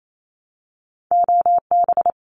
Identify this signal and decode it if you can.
Morse code, heard as a keyed tone, about 28 wpm, 700 Hz. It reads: OB
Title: Abbreviations Repeated 3x - 28wpm